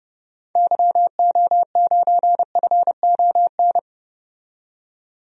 Morse code, heard as a keyed tone, about 30 wpm, 700 Hz. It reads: YO9FON
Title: DXCC-100 Call Signs - 6 Characters - 30wpm